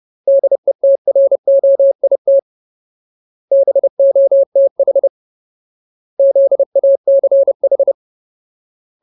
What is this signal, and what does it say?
Morse code, heard as a keyed tone, about 30 wpm, 550 Hz. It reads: DETROIT BOTH ZACH
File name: Ultimate Word List / 30WPM Ultimate